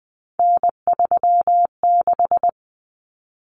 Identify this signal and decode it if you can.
Morse code, heard as a keyed tone, about 20 wpm, 700 Hz. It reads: N36
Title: Common Cut Numbers - 3-Digit 20wpm